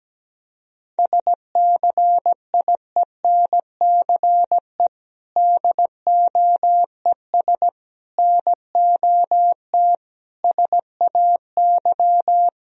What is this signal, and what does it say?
Morse code, heard as a keyed tone, about 17 wpm, 700 Hz. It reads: SCIENCE DOES NOT SAY
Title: Sentences from Top 1000 Words - Review Part 2 - 17wpm